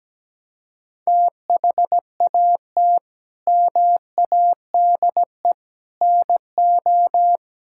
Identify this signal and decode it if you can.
Morse code, heard as a keyed tone, about 17 wpm, 700 Hz. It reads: THAT MADE NO
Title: Sentences from Top 1000 Words - Review Part 5 - 17wpm